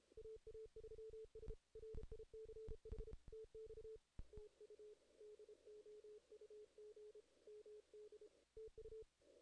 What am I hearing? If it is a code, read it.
Morse code, heard as a keyed tone, about 32 wpm, 440 Hz. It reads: AA3S RICH TX TU DOUG MD TU
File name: 2016 JAN CW NAQP / 2016 Jan naqp k5ot - 0-8